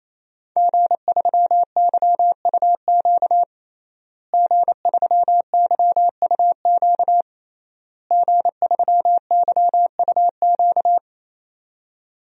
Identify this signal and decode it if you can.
Morse code, heard as a keyed tone, about 28 wpm, 700 Hz. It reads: G3YUQ G3YUQ G3YUQ
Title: DXCC-100 Call Signs - 5 Characters - Repeated 3x - 28wpm